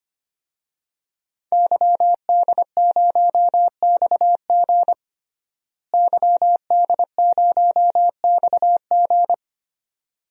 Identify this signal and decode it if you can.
Morse code, heard as a keyed tone, about 25 wpm, 700 Hz. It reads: YD0XG YD0XG
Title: DXCC-100 Call Signs - 5 Characters - Repeated 2x - 25wpm